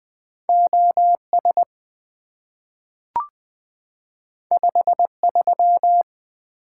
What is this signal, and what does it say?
Morse code, heard as a keyed tone, about 20 wpm, 700 Hz. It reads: OS 53
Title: Two Letter-Number Combinations - 20wpm